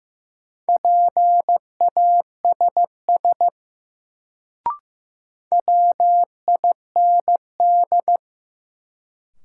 Morse code, heard as a keyed tone, about 15 wpm, 700 Hz. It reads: PASS WIND
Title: Sentences from Top 400 Words - New Word Drill - 15wpm